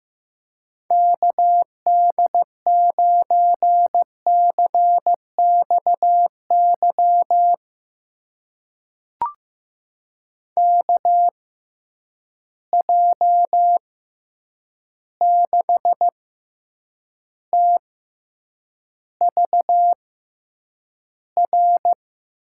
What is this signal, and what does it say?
Morse code, heard as a keyed tone, about 15 wpm, spaced as for 8 wpm, 700 Hz. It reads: KD9CXY KJ6TVR
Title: USA Call Signs - 6 Characters - 6x Character Spacing - 15wpm